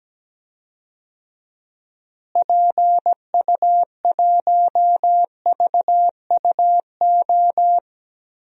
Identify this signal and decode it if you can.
Morse code, heard as a keyed tone, about 17 wpm, 700 Hz. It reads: PU1VUO
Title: DXCC-100 Call Signs - 6 Characters - Repeated 3x - 17wpm